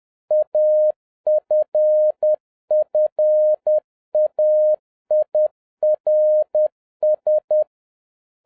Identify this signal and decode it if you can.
Morse code, heard as a keyed tone, about 10 wpm, 610 Hz. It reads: AFFAIRS